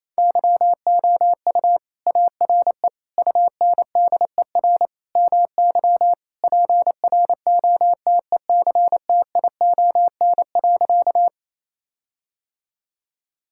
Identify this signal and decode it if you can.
Morse code, heard as a keyed tone, about 28 wpm, 700 Hz. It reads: YOU ARE UNDER MY PROTECTION.
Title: Sentences from Top 1600 Words 28wpm